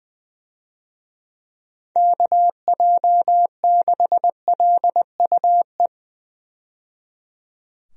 Morse code, heard as a keyed tone, about 20 wpm, 700 Hz. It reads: KJ6LUE
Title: USA Call Signs - 6 Characters - 20wpm